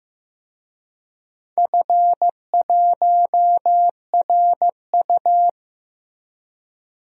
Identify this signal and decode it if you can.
Morse code, heard as a keyed tone, about 15 wpm, 700 Hz. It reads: F1RU
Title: DXCC-100 Call Signs - 4 Characters - Repeated 3x - 15wpm